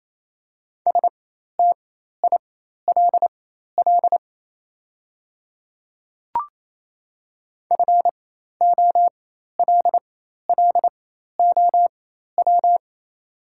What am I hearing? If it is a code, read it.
Morse code, heard as a keyed tone, about 28 wpm, spaced as for 13 wpm, 700 Hz. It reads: STILL FOLLOW
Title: 4x Character Spacing - Top 300 Words - 28wpm